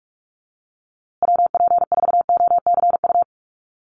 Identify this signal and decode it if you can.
Morse code, heard as a keyed tone, about 44 wpm, 690 Hz. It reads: WP4OCU
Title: USA Call Signs Repeated 3x 45wpm